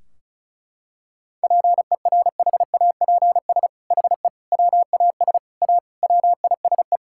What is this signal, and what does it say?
Morse code, heard as a keyed tone, about 34 wpm, 700 Hz. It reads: PERHAPS HE WAS A WISE
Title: Sentences from Top 1600 Words 35wpm